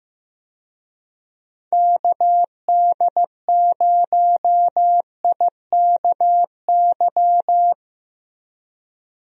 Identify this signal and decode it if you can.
Morse code, heard as a keyed tone, about 15 wpm, 700 Hz. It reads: KD0IKY
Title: USA Call Signs - 6 Characters - Repeated 3x - 15wpm